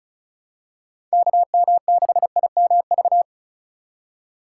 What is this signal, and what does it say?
Morse code, heard as a keyed tone, about 36 wpm, 700 Hz. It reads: KM6IMV
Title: USA Call Signs - 6 Characters - 2x Character Spacing - 35wpm